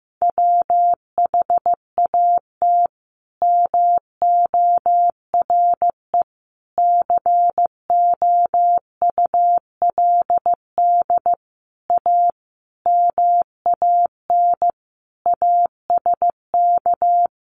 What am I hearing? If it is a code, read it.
Morse code, heard as a keyed tone, about 15 wpm, 700 Hz. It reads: WHAT MORE COULD A MAN ASK
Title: Sentences from Top 200 Words 15wpm